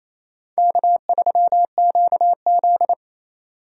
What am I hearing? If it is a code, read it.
Morse code, heard as a keyed tone, about 34 wpm, 700 Hz. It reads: K3QZ
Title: Speed Racing - USA 1x2 Call Signs 28wpm